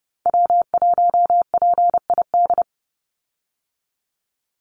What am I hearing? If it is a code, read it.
Morse code, heard as a keyed tone, about 30 wpm, 700 Hz. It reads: W1PID
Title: Call Signs Repeated 3x - 30wpm